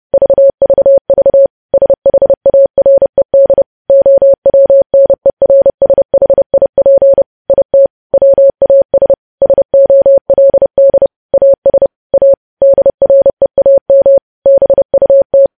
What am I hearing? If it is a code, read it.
Morse code, heard as a keyed tone, about 30 wpm, 550 Hz. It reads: VVV SHARED OWNERSHIP IT WAS SOLD AS A DREAM BUT
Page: news headlines in morse code at 30 wpm